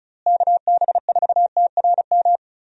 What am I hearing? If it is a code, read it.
Morse code, heard as a keyed tone, about 36 wpm, 700 Hz. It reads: KB4TRM
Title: USA Call Signs - 6 Characters - Repeated 2x - 35wpm